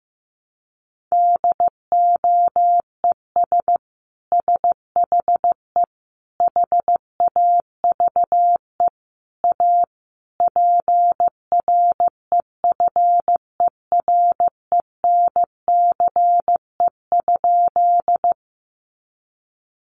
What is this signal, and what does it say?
Morse code, heard as a keyed tone, about 15 wpm, 700 Hz. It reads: DOES SHE HAVE A PREFERENCE?
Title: Sentences from Top 1500 Words 15wpm